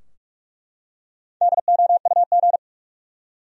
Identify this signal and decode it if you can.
Morse code, heard as a keyed tone, about 46 wpm, 700 Hz. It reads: DOUG
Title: USA Common Names - Max 5 Letters - 45wpm